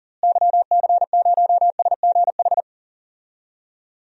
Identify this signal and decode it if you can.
Morse code, heard as a keyed tone, about 38 wpm, 700 Hz. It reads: YC0SGH
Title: DXCC-100 Call Signs - 6 Characters - Repeated 3x - 40wpm